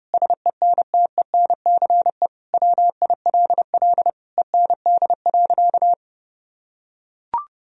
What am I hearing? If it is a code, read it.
Morse code, heard as a keyed tone, about 32 wpm, 700 Hz. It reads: SENTENCE WILL END.
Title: Sentences from Top 1000 Words - Review Part 1 - 30wpm